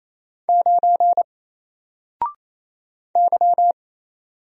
Morse code, heard as a keyed tone, about 27 wpm, 700 Hz. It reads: 9 Y
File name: Single-Letter-Number-28wpm